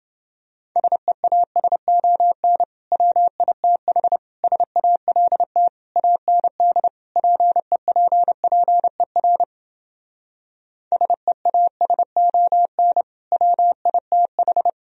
Morse code, heard as a keyed tone, about 30 wpm, 700 Hz. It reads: SEASON WITH SALT AND PEPPER SEASON WITH